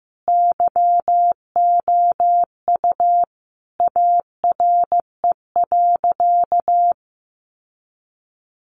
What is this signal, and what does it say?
Morse code, heard as a keyed tone, about 15 wpm, 700 Hz. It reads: YOU ARE.
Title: Sentences from Top 1200 Words 15wpm